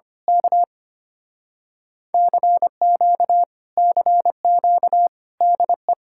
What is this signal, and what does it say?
Morse code, heard as a keyed tone, about 24 wpm, 700 Hz. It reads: K CQ CQ DE
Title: Speed-Racing Common QSO Phrases - 25wpm